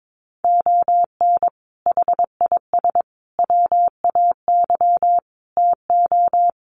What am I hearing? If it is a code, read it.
Morse code, heard as a keyed tone, about 22 wpm, 700 Hz. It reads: ON HIS WAY TO
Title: Sets of 5 Words Encore2 22wpm